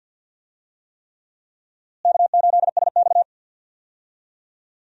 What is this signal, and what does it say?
Morse code, heard as a keyed tone, about 52 wpm, 700 Hz. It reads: K8SX
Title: USA Call Signs - 4 Characters - Repeated 3x - 50wpm